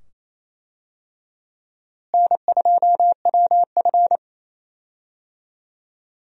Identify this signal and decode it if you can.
Morse code, heard as a keyed tone, about 27 wpm, 700 Hz. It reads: N2WF
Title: USA Call Signs - 4 Characters - Repeated 2x - 28wpm